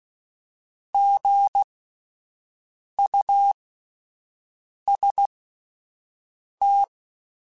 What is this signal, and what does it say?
Morse code, heard as a keyed tone, about 16 wpm, spaced as for 5 wpm, 800 Hz. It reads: GUST
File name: Spaced Alphas / Spaced Alphas mp3s / Gs sp @10